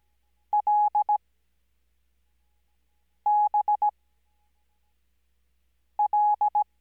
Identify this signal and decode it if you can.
Morse code, heard as a keyed tone, about 17 wpm, spaced as for 4 wpm, 840 Hz. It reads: LBL
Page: Farnsworth Spacing: 15WPM Character speed Spaced at 5WPM.